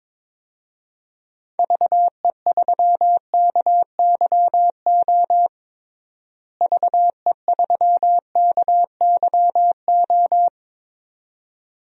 Morse code, heard as a keyed tone, about 21 wpm, 700 Hz. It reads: VE3KYO VE3KYO
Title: DXCC-100 Call Signs - 6 Characters - Repeated 2x - 22wpm